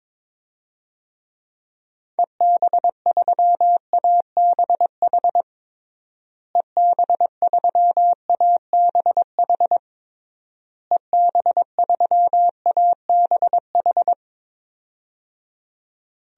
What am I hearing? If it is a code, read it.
Morse code, heard as a keyed tone, about 21 wpm, 700 Hz. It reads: EB3ABH EB3ABH EB3ABH
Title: DXCC-100 Call Signs - 6 Characters - Repeated 3x - 22wpm